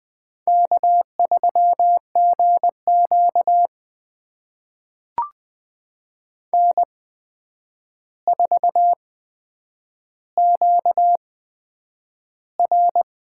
Morse code, heard as a keyed tone, about 20 wpm, spaced as for 9 wpm, 700 Hz. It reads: K3GQ N4QR
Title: USA Call Signs - 4 Characters - 8x Character Spacing - 20wpm